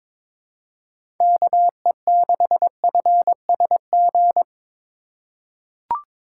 Morse code, heard as a keyed tone, about 21 wpm, 700 Hz. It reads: KE6FSG
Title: USA Call Signs - 6 Characters - 4x Character Spacing - 22wpm